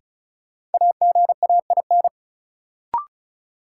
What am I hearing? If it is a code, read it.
Morse code, heard as a keyed tone, about 36 wpm, 700 Hz. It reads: AGAIN
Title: Sets of 4 Words - New Word Drill - 35wpm